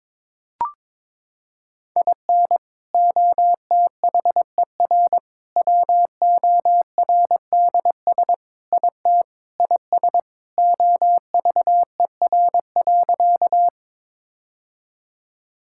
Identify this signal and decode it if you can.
Morse code, heard as a keyed tone, about 22 wpm, 700 Hz. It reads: IN OTHER WORDS IT IS OVER.
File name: practice / 22wpm / Sentences with 3-Word Phrases - 22wpm